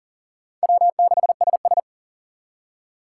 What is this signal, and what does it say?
Morse code, heard as a keyed tone, about 50 wpm, 700 Hz. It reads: W6SS
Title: Speed Racing - USA 1x2 Call Signs 40wpm